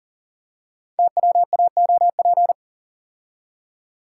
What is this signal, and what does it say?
Morse code, heard as a keyed tone, about 42 wpm, 700 Hz. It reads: TWAOP
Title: Code Groups - 5 Letters - 40wpm